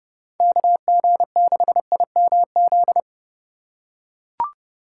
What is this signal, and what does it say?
Morse code, heard as a keyed tone, about 29 wpm, 700 Hz. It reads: KG6IMZ